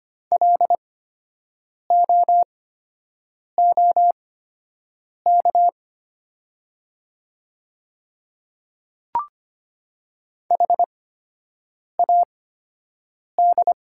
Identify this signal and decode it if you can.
Morse code, heard as a keyed tone, about 25 wpm, spaced as for 7 wpm, 700 Hz. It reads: LOOK HAD
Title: 8x Character Spacing - Top 100 Words - 25wpm